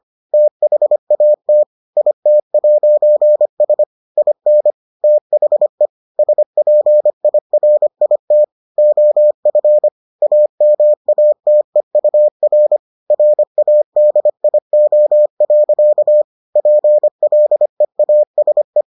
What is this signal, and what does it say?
Morse code, heard as a keyed tone, about 25 wpm, 600 Hz. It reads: THAT IT'S IN THE SPIRIT OF AMATEUR RADIO. PLEASE